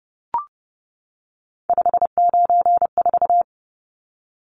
Morse code, heard as a keyed tone, about 30 wpm, 700 Hz. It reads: 594
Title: Numbers - 3-Digit 30wpm